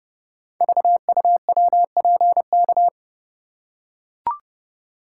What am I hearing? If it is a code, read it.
Morse code, heard as a keyed tone, about 30 wpm, 700 Hz. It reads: VUWPK
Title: Code Groups - 5 Letters - 30wpm